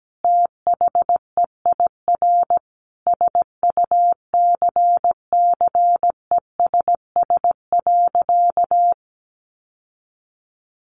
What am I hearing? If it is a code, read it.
Morse code, heard as a keyed tone, about 17 wpm, 700 Hz. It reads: THEIR SUCCESS.
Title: Sentences from Top 900 Words 17wpm